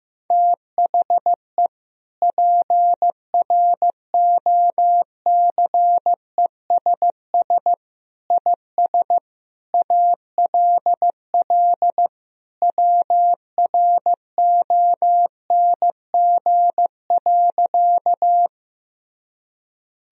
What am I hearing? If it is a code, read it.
Morse code, heard as a keyed tone, about 15 wpm, 700 Hz. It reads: THE PROCESS IS ALL WRONG.
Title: Sentences from Top 1000 Words - Review Part 3 - 15wpm